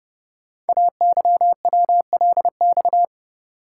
Morse code, heard as a keyed tone, about 29 wpm, 700 Hz. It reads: AYWLX
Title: Code Groups - 5 Letters - 30wpm